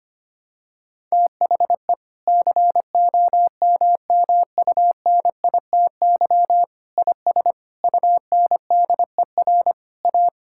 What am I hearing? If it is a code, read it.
Morse code, heard as a keyed tone, about 25 wpm, 700 Hz. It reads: THE COMMUNITY IS UNDER A
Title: Sentences from Top 1000 Words - Review Part 6 - 25wpm